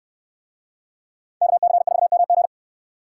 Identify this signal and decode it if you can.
Morse code, heard as a keyed tone, about 66 wpm, 700 Hz. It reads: KC4MG